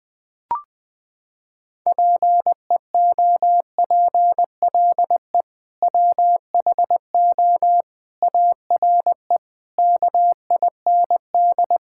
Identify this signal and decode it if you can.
Morse code, heard as a keyed tone, about 20 wpm, 700 Hz. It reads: PEOPLE WHO ARE KIND
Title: Sentences with 3-Word Phrases - 20wpm